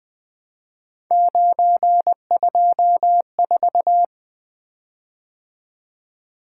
Morse code, heard as a keyed tone, about 21 wpm, 700 Hz. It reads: 924
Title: Three Letter-Number Combinations - 20wpm